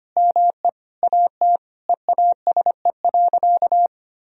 Morse code, heard as a keyed tone, about 25 wpm, 700 Hz. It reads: ME AT EASE.
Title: Sentences from Top 1000 Words - Review Part 1 - 25wpm